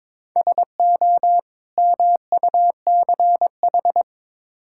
Morse code, heard as a keyed tone, about 22 wpm, 700 Hz. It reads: SO MUCH
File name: Build A Phrase - 2-Word Phrases - 22wpm